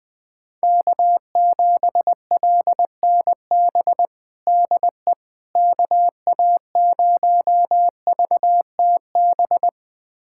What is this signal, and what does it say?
Morse code, heard as a keyed tone, about 20 wpm, 700 Hz. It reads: K7LNB DE KA0VTB
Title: USA Callsign DE Callsign 20wpm